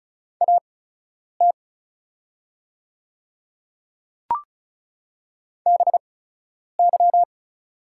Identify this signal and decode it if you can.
Morse code, heard as a keyed tone, about 34 wpm, spaced as for 10 wpm, 700 Hz. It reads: AT BY
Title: 8x Character Spacing - Top 300 Words - 35wpm